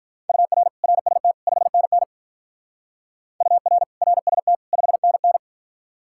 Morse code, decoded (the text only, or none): UR RST 5NN UR RST 5NN